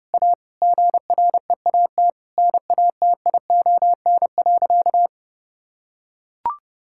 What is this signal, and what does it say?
Morse code, heard as a keyed tone, about 30 wpm, 700 Hz. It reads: A GREAT NATION.
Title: Sentences from Top 1000 Words - Review Part 6 - 30wpm